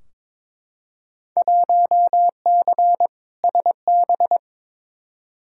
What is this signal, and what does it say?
Morse code, heard as a keyed tone, about 22 wpm, 700 Hz. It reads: 1C SB
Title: Field Day Exchanges Encore - 22wpm